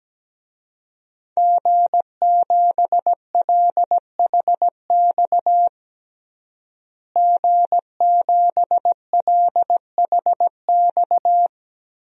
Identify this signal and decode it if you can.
Morse code, heard as a keyed tone, about 17 wpm, 700 Hz. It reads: G7LHX G7LHX
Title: DXCC-100 Call Signs - 5 Characters - Repeated 3x - 17wpm